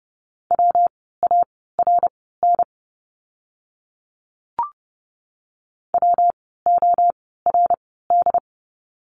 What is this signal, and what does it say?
Morse code, heard as a keyed tone, about 30 wpm, spaced as for 17 wpm, 700 Hz. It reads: WARN WORD
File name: course / 30:10wpm / 115 Lesson 19f - Letter w - Words 30wpm